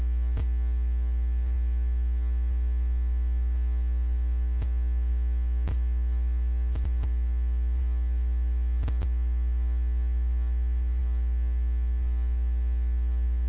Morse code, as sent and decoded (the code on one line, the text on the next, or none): none